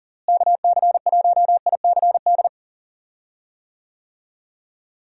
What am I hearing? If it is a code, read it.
Morse code, heard as a keyed tone, about 39 wpm, 700 Hz. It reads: KC1ICD
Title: USA Call Signs - 6 Characters - Repeated 2x - 40wpm